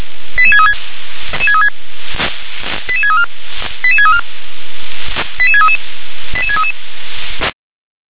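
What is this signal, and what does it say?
Five-tone selective call (ZVEI), heard as quick runs of five tones, tones of about 70 ms. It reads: 88636 88634 88634 88633 88633
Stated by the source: ZVEI five-tone selective call received on an SDR.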